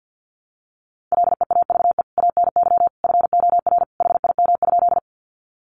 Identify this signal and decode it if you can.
Morse code, heard as a keyed tone, about 49 wpm, 700 Hz. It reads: LEAVE ANY FOR HIM?
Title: Sentences from Top 400 Words 50wpm